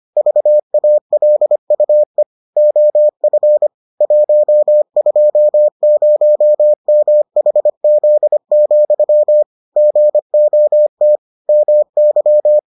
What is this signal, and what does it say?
Morse code, heard as a keyed tone, about 25 wpm, 600 Hz. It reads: VALUE OF 120MHZ, GOT MY